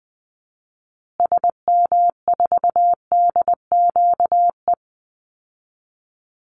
Morse code, heard as a keyed tone, about 20 wpm, 700 Hz. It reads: SM4DQE
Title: Call Signs Repeated 3x - 20wpm